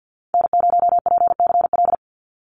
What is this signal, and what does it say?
Morse code, heard as a keyed tone, about 50 wpm, 700 Hz. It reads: N0PCL
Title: NA SOTA Activator Callsigns 50wpm